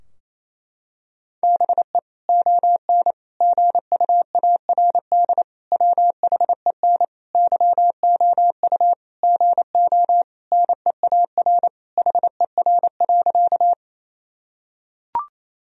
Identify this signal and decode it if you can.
Morse code, heard as a keyed tone, about 27 wpm, 700 Hz. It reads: BE ON GUARD WHEN YOU GO NEAR HER.
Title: Sentences from Top 1000 Words - Review Part 4 - 28wpm